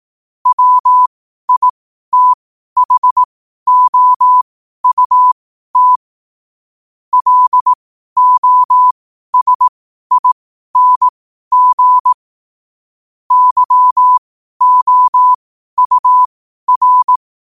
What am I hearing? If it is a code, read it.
Morse code, heard as a keyed tone, about 18 wpm, spaced as for 12 wpm, 1000 Hz. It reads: WITHOUT LOSING YOUR
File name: qotdfiles / 12wpm